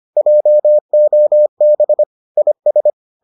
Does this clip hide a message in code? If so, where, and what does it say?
Yes: Morse code from 0.2 to 2.9 s, heard as a keyed tone, about 25 wpm, 600 Hz. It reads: JOB IS